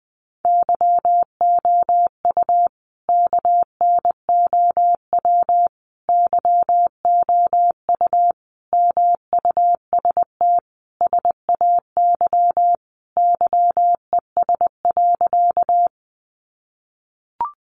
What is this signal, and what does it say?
Morse code, heard as a keyed tone, about 20 wpm, 700 Hz. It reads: YOU KNOW YOU MUST SAY YES.
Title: Sentences from Top 500 Words 20wpm